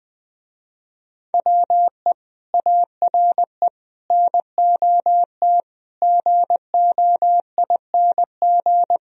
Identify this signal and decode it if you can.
Morse code, heard as a keyed tone, about 20 wpm, 700 Hz. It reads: WE ARE NOT GOING
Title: Speed-Racing - Sets of 5 Words 20wpm